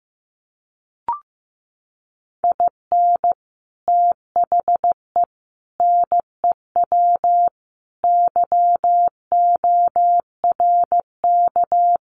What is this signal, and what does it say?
Morse code, heard as a keyed tone, about 15 wpm, 700 Hz. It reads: IN THE NEW YORK
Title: Sets of 4 Words 15wpm